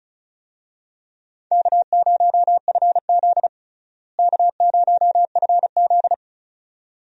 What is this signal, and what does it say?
Morse code, heard as a keyed tone, about 36 wpm, 700 Hz. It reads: K0FZ K0FZ
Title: USA Call Signs - 4 Characters - Repeated 2x - 35wpm